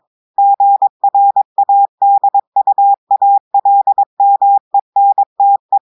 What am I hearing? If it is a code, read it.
Morse code, heard as a keyed tone, about 23 wpm, 800 Hz. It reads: GRADUALMENTE
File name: audios / portugais / portugais texte en clair 2 vitesse 22 mots minutes